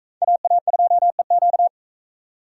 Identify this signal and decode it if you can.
Morse code, heard as a keyed tone, about 42 wpm, 700 Hz. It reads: AA2EQ